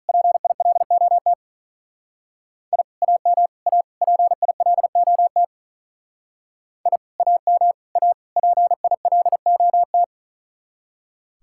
Morse code, heard as a keyed tone, about 41 wpm, 700 Hz. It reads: PILOT I AM A PILOT I AM A PILOT
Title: Speed-Racing Common QSO Phrases Encore - 35wpm